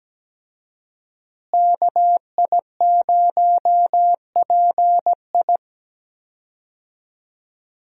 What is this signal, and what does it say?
Morse code, heard as a keyed tone, about 17 wpm, 700 Hz. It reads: KI0PI
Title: USA Call Signs - 5 Characters - Repeated 2x - 17wpm